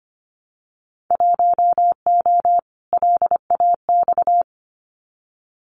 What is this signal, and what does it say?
Morse code, heard as a keyed tone, about 25 wpm, 700 Hz. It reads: 1O LAX
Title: Winter Field Day Exchanges 25wpm